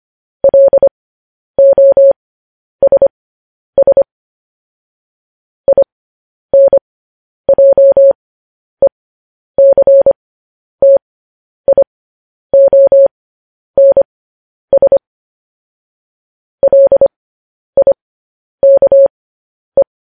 Morse code, heard as a keyed tone, about 25 wpm, spaced as for 10 wpm, 550 Hz. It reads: LOSS INJECTIONS LIKE
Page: News Headlines in Morse Code at 10 WPM